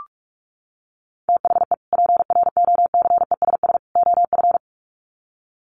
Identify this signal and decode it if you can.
Morse code, heard as a keyed tone, about 44 wpm, 700 Hz. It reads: THE PROCESS OF
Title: Sets of 3 Words 45wpm